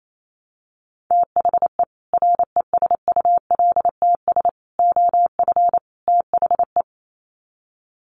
Morse code, heard as a keyed tone, about 28 wpm, 700 Hz. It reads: THE RESULTS OF THE
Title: Sets of 4 Words 28wpm